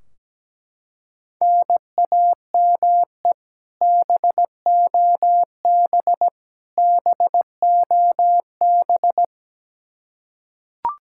Morse code, heard as a keyed tone, about 17 wpm, 700 Hz. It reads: NAME BOB BOB
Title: K1USN SST QSO Practice 17wpm